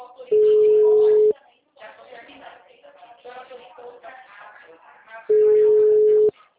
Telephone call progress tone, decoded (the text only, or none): ringback tone